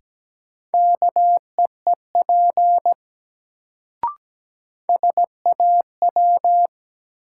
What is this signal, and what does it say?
Morse code, heard as a keyed tone, about 17 wpm, 700 Hz. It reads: KEEP SAW
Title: Sentences from Top 300 Words - New Word Drill - 17wpm